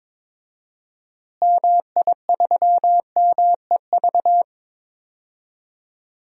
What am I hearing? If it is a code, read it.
Morse code, heard as a keyed tone, about 22 wpm, 700 Hz. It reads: MI3MEV
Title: DXCC-100 Call Signs - 6 Characters - Repeated 2x - 22wpm